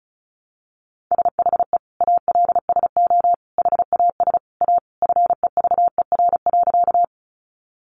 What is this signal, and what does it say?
Morse code, heard as a keyed tone, about 34 wpm, 700 Hz. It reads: SHE ALSO HAS A FEVER.
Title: Easy Phrases 35wpm